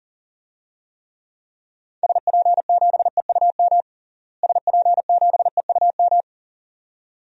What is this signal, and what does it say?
Morse code, heard as a keyed tone, about 40 wpm, 700 Hz. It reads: SP7EUM SP7EUM
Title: DXCC-100 Call Signs - 6 Characters - Repeated 2x - 40wpm